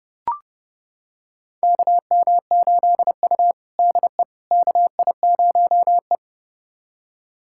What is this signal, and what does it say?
Morse code, heard as a keyed tone, about 30 wpm, 700 Hz. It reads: KM8U DE KI0E
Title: USA Callsign DE Callsign - 4 Characters - 30wpm